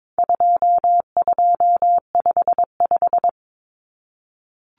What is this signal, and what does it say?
Morse code, heard as a keyed tone, about 22 wpm, 700 Hz. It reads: 2255
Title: Numbers - 4-Digit 22wpm